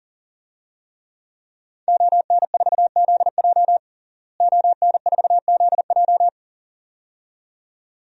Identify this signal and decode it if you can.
Morse code, heard as a keyed tone, about 40 wpm, 700 Hz. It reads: ON4ZJ ON4ZJ
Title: DXCC-100 Call Signs - 5 Characters - Repeated 2x - 40wpm